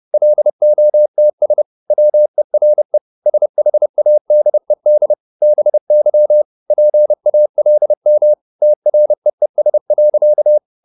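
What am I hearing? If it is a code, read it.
Morse code, heard as a keyed tone, about 30 wpm, 600 Hz. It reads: LOTS WERE SHADED BY PALM TREES.
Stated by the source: Morse code at 29 wpm, sending Ken Follett's novel Never in English, from CW med Gnister.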